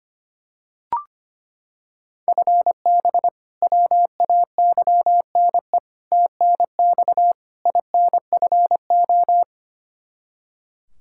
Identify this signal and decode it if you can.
Morse code, heard as a keyed tone, about 25 wpm, 700 Hz. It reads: FB WAYNE TNX INFO
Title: K1USN SST QSO Practice 25wpm